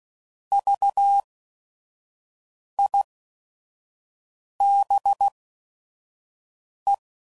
Morse code, heard as a keyed tone, about 16 wpm, spaced as for 5 wpm, 800 Hz. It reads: VIBE
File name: Prefixes / By First Letter / Vs